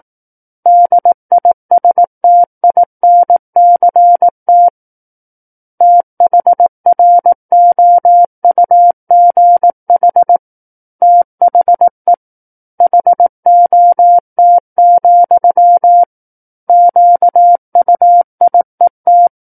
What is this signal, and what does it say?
Morse code, heard as a keyed tone, about 18 wpm, 700 Hz. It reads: DISTINCT THROUGH THE HOT, QUIET
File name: Book One / 09-Chapter 9 - 18 WPM